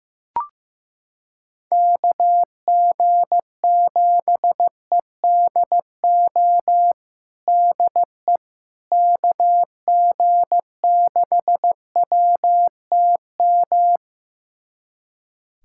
Morse code, heard as a keyed tone, about 15 wpm, 700 Hz. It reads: KG7EDO DE KG6WTM
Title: USA Callsign DE Callsign 15wpm